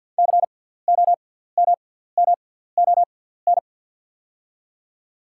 Morse code, heard as a keyed tone, about 48 wpm, spaced as for 17 wpm, 690 Hz. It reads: COMMON
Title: 6x Character Spacing - Top 500 Words - 50wpm